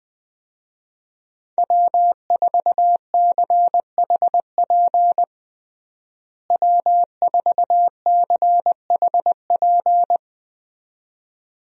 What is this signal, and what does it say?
Morse code, heard as a keyed tone, about 20 wpm, 700 Hz. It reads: W4CHP W4CHP
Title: USA Call Signs - 5 Characters - Repeated 2x - 20wpm